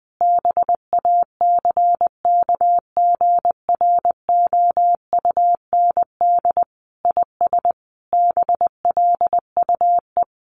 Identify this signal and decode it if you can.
Morse code, heard as a keyed tone, about 19 wpm, 700 Hz. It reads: BACKGROUND IS BLUE
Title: Sentences from Top 1300 Words 20wpm